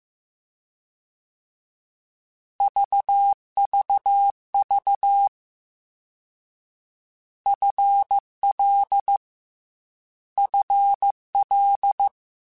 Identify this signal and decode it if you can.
Morse code, heard as a keyed tone, about 15 wpm, 800 Hz. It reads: VVV FL FL
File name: Combinations / Combinations-mp3s / Combinations@13